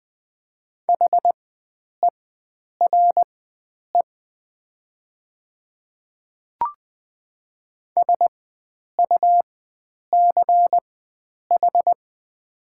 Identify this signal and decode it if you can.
Morse code, heard as a keyed tone, about 20 wpm, spaced as for 9 wpm, 700 Hz. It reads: HERE SUCH
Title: 4x Character Spacing - Top 500 Words - 20wpm